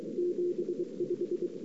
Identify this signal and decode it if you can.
Morse code, heard as a keyed tone, about 23 wpm, 350 Hz. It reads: 75
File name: CW / 2008-ss-cw-k5tr-op-n5rz -02:00